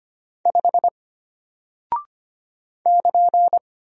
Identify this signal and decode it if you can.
Morse code, heard as a keyed tone, about 25 wpm, 700 Hz. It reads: 5 <KN>
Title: Single Character (Letter:Number:Punctuation:Prosign) Alt 25wpm